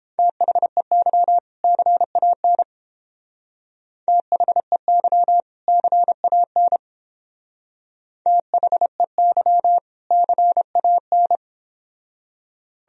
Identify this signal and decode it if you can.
Morse code, heard as a keyed tone, about 29 wpm, 700 Hz. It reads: THEY CAN THEY CAN THEY CAN